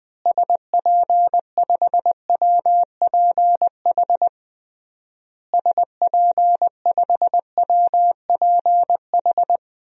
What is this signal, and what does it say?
Morse code, heard as a keyed tone, about 20 wpm, 700 Hz. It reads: SP5WPH SP5WPH
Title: DXCC-100 Call Signs - 6 Characters - Repeated 3x - 20wpm